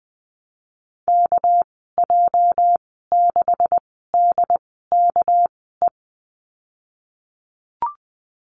Morse code, heard as a keyed tone, about 20 wpm, spaced as for 15 wpm, 700 Hz. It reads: KJ6DKE
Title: 213 Lesson 34g - Number 8 - Call Signs 20wpm